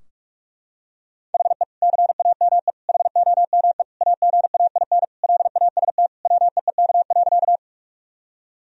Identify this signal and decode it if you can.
Morse code, heard as a keyed tone, about 45 wpm, 700 Hz. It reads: HE CAME HOME AGAIN LAST WEEK.
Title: Sentences from Top 1000 Words - Review Part 2 - 45wpm